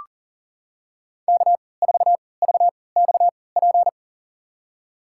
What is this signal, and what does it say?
Morse code, heard as a keyed tone, about 40 wpm, spaced as for 23 wpm, 700 Hz. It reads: K4VXP
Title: USA Call Signs - 5 Characters - 3x Character Spacing - 40wpm